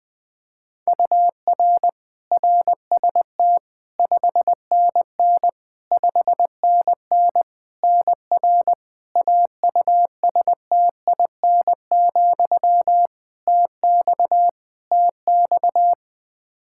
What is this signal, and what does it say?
Morse code, heard as a keyed tone, about 20 wpm, 700 Hz. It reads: UR RST 5NN 5NN NR AUSTIN, TX TX
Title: Practice QSO - Exchange 1 - Beginning - 20wpm